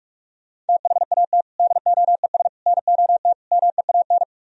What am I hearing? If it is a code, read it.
Morse code, heard as a keyed tone, about 58 wpm, 700 Hz. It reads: THAT DOES NOT MEAN